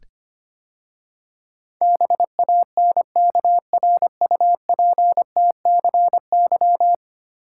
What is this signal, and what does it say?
Morse code, heard as a keyed tone, about 25 wpm, 690 Hz. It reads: BANKRUPTCY
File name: 10 Letter Words - First 3 Letters Given - 25wpm